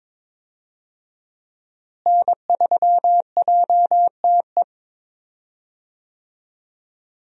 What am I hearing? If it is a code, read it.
Morse code, heard as a keyed tone, about 21 wpm, 700 Hz. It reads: N3JTE
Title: USA Call Signs - 5 Characters - Repeated 3x - 22wpm